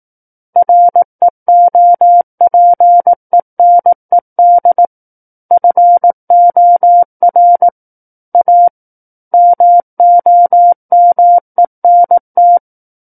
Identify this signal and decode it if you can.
Morse code, heard as a keyed tone, about 18 wpm, 700 Hz. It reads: REOPENED FOR A MOMENT
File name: Book One / 09-Chapter 9 - 18 WPM